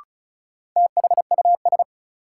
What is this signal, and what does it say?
Morse code, heard as a keyed tone, about 34 wpm, 700 Hz. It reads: THUS
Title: Sentences from Top 800 Words - New Word Drill - 35wpm